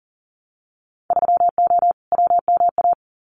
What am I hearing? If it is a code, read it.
Morse code, heard as a keyed tone, about 40 wpm, 700 Hz. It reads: 3O WMA
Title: Winter Field Day Exchanges Encore 40wpm